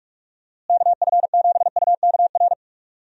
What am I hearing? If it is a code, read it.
Morse code, heard as a keyed tone, about 43 wpm, 700 Hz. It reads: KF7UKR
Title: USA Call Signs - 6 Characters - Repeated 2x - 45wpm